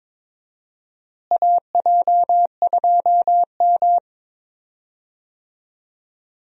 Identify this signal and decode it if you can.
Morse code, heard as a keyed tone, about 22 wpm, 700 Hz. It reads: AJ2M
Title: USA Call Signs - 4 Characters - Repeated 3x - 22wpm